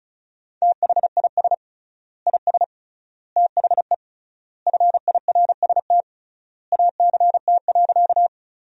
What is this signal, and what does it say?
Morse code, heard as a keyed tone, about 34 wpm, 700 Hz. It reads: THIS IS THE FIRST ACT.
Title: 3x Word-Spacing - Sentences from Top 200 Words - 35wpm